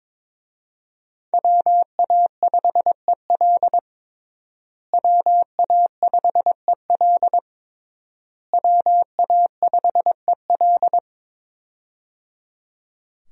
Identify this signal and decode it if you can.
Morse code, heard as a keyed tone, about 22 wpm, 700 Hz. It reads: WA5EL WA5EL WA5EL
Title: USA Call Signs - 5 Characters - Repeated 3x - 22wpm